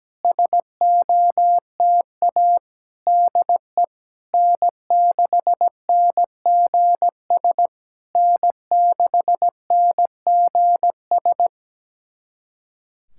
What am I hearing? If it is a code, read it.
Morse code, heard as a keyed tone, about 17 wpm, 700 Hz. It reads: SOTA DE N6NGS N6NGS
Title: SOTA QSO Practice 17wpm